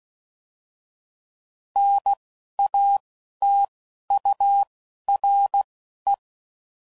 Morse code, heard as a keyed tone, about 16 wpm, spaced as for 11 wpm, 800 Hz. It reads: NATURE
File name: Prefixes / By First Letter / Ns